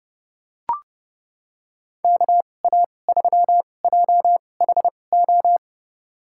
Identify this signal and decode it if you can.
Morse code, heard as a keyed tone, about 32 wpm, spaced as for 22 wpm, 700 Hz. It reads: KA3JHO
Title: USA Call Signs - 2x Character Spacing - 30wpm